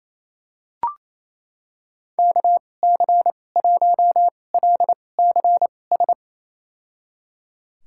Morse code, heard as a keyed tone, about 28 wpm, spaced as for 20 wpm, 700 Hz. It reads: KC1LCS
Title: USA Call Signs - 2x Character Spacing - 28wpm